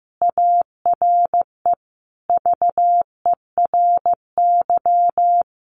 Morse code, heard as a keyed tone, about 15 wpm, 700 Hz. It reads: ARE VERY
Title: Sentences from Top 1200 Words 15wpm